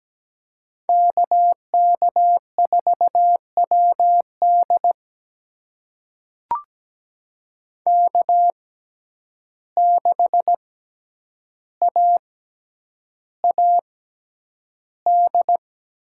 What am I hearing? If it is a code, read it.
Morse code, heard as a keyed tone, about 17 wpm, spaced as for 9 wpm, 700 Hz. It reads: KK4WD K6AAD
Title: USA Call Signs - 5 Characters - 6x Character Spacing - 17wpm